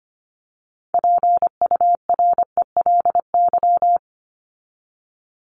Morse code, heard as a keyed tone, about 25 wpm, 700 Hz. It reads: PURELY
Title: LY Suffix - Top 100 Words with Base Words 25wpm